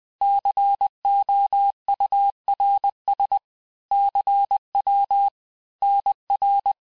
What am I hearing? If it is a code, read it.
Morse code, heard as a keyed tone, about 20 wpm, 800 Hz. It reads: COURS CW NR
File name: cours-858-texte-chiffre-vitesse-19mpm